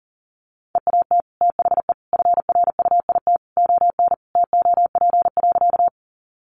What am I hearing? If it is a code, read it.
Morse code, heard as a keyed tone, about 40 wpm, 700 Hz. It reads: EAT THE FRUIT ON TOP.
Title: Sentences from Top 1300 Words 40wpm